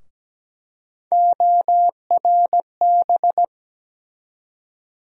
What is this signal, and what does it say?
Morse code, heard as a keyed tone, about 17 wpm, 700 Hz. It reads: ORB
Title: USA Call Sign Postfixes - 3 Letters 17wpm